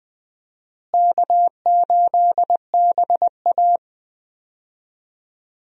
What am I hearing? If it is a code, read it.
Morse code, heard as a keyed tone, about 21 wpm, 700 Hz. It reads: K8BA
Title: USA Call Signs - 4 Characters - Repeated 3x - 20wpm